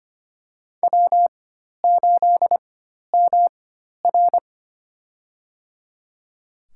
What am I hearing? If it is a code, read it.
Morse code, heard as a keyed tone, about 25 wpm, spaced as for 12 wpm, 700 Hz. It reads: W8MR
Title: USA Call Signs - 4 Characters - 4x Character Spacing - 25wpm